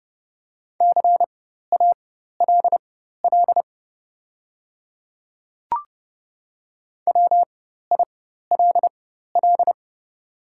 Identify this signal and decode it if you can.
Morse code, heard as a keyed tone, about 30 wpm, spaced as for 14 wpm, 700 Hz. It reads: CALL WILL
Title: 4x Character Spacing - Top 100 Words - 30wpm